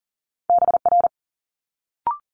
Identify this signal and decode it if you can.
Morse code, heard as a keyed tone, about 38 wpm, 700 Hz. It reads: BR
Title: Multi-letter Phonemes Quick Fire 40wpm